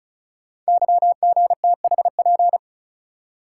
Morse code, heard as a keyed tone, about 36 wpm, 690 Hz. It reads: YGTHP
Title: Code Groups - 5 Letters - 35wpm